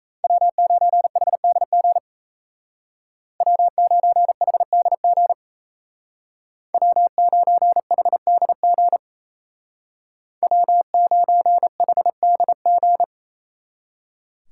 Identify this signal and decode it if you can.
Morse code, heard as a keyed tone, about 34 wpm, 700 Hz. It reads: W9HDG W9HDG W9HDG W9HDG
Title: Speed Racing - USA 1x3 Call Signs 28wpm